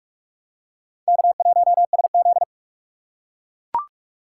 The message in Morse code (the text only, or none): K1SZ